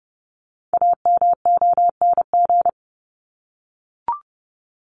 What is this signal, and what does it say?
Morse code, heard as a keyed tone, about 30 wpm, 700 Hz. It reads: AMONG
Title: Top 500 Words Repeated 30wpm